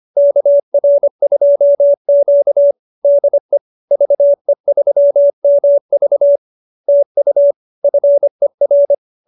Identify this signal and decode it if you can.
Morse code, heard as a keyed tone, about 25 wpm, 590 Hz. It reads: KR2Q DE VE3MV TU FER